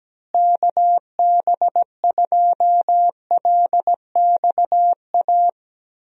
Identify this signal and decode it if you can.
Morse code, heard as a keyed tone, about 17 wpm, 700 Hz. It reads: KB2LXA
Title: USA Call Signs - 6 Characters - Repeated 3x - 17wpm